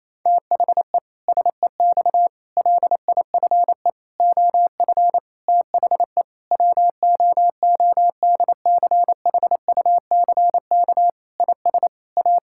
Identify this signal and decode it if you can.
Morse code, heard as a keyed tone, about 28 wpm, 700 Hz. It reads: THE SEX LIFE OF THE WOODCHUCK IS A
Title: Pangram Challenge v2 - 28wpm